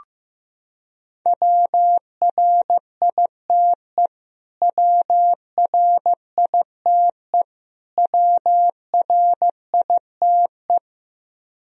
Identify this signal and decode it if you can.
Morse code, heard as a keyed tone, about 15 wpm, 700 Hz. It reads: WRITE WRITE WRITE
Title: Top 100 Words Repeated 15wpm